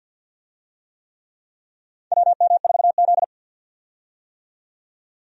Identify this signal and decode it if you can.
Morse code, heard as a keyed tone, about 52 wpm, 700 Hz. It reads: WM4Z
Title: USA Call Signs - 4 Characters - 50wpm